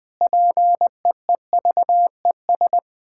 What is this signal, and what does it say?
Morse code, heard as a keyed tone, about 20 wpm, 700 Hz. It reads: PEEVES
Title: Pangram Challenge v2 - 20wpm